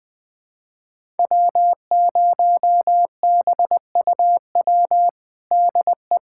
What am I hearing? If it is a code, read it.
Morse code, heard as a keyed tone, about 21 wpm, 700 Hz. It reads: W0BUW DE
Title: USA Callsign DE Callsign - 5 Characters - 20wpm